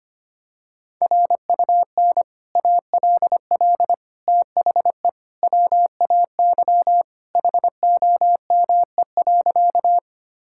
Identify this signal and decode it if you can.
Morse code, heard as a keyed tone, about 25 wpm, 700 Hz. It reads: RUN ALL THE WAY HOME.
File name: practice / 25wpm / Sentences with 3-Word Phrases - 25wpm